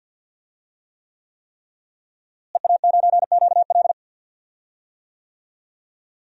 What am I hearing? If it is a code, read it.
Morse code, heard as a keyed tone, about 46 wpm, 700 Hz. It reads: EA9QL
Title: DXCC-100 Call Signs - 5 Characters - Repeated 3x - 50wpm